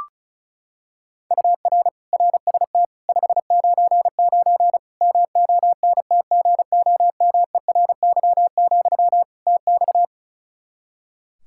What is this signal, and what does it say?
Morse code, heard as a keyed tone, about 36 wpm, 700 Hz. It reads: UR RST 599 MONTGOMERY, TX
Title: K1USN SST QSO Practice 35wpm